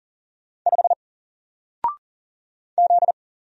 Morse code, heard as a keyed tone, about 39 wpm, 700 Hz. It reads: 5 Z
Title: Single Character (Letter:Number:Punctuation:Prosign) 40wpm